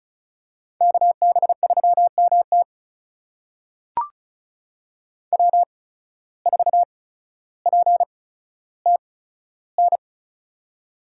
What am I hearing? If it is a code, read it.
Morse code, heard as a keyed tone, about 34 wpm, spaced as for 15 wpm, 700 Hz. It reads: KB3MT W4PTN
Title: USA Call Signs - 5 Characters - 8x Character Spacing - 35wpm